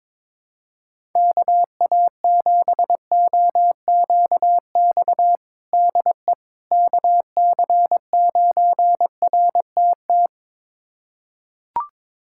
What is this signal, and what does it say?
Morse code, heard as a keyed tone, about 22 wpm, 700 Hz. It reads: KA7OQX DE KC9RTT
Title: USA Callsign DE Callsign - 6 Characters - 22wpm